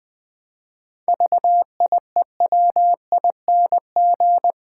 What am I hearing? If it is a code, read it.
Morse code, heard as a keyed tone, about 20 wpm, 700 Hz. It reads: VIEWING
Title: Pangram Challenge v2 - 20wpm